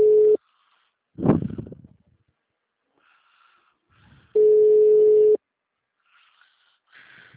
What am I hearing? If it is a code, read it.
Ringback tone (Europe, 425 Hz).
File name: out- -2007- - - .13134